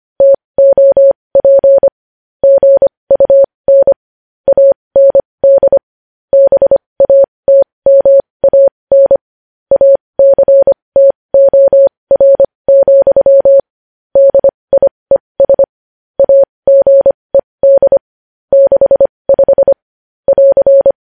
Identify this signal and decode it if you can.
Morse code, heard as a keyed tone, about 25 wpm, spaced as for 20 wpm, 550 Hz. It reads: TOP GUN AND BATMAN ACTOR, DIES AGED 65 <AR>
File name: podcast / News headlines-1x-25x20